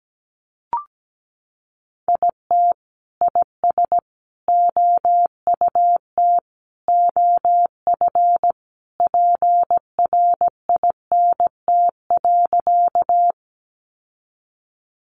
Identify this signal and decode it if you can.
Morse code, heard as a keyed tone, about 17 wpm, 700 Hz. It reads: IT IS OUT OF PRINT.
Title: Sentences from Top 1400 Words 17wpm